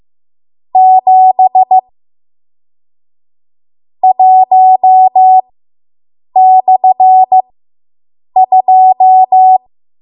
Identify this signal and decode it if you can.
Morse code, heard as a keyed tone, about 15 wpm, spaced as for 7 wpm, 750 Hz. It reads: 7 1/2